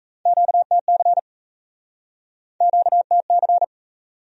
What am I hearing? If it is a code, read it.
Morse code, heard as a keyed tone, about 40 wpm, 700 Hz. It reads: QTC QTC